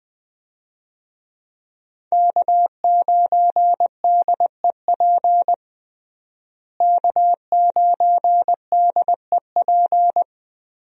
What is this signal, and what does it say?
Morse code, heard as a keyed tone, about 20 wpm, 700 Hz. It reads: K9DEP K9DEP
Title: USA Call Signs - 5 Characters - Repeated 3x - 20wpm